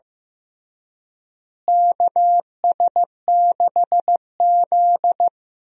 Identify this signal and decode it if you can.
Morse code, heard as a keyed tone, about 15 wpm, 700 Hz. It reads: KS6Z
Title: USA Call Signs - 4 Characters - Repeated 2x - 15wpm